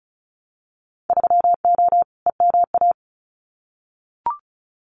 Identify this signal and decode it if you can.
Morse code, heard as a keyed tone, about 35 wpm, 700 Hz. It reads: 3O EMA
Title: Winter Field Day Exchanges Encore 35wpm